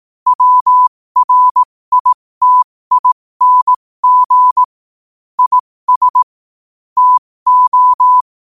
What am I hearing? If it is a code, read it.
Morse code, heard as a keyed tone, about 18 wpm, spaced as for 15 wpm, 1000 Hz. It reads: WRITING IS TO
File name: qotdfiles / 15wpm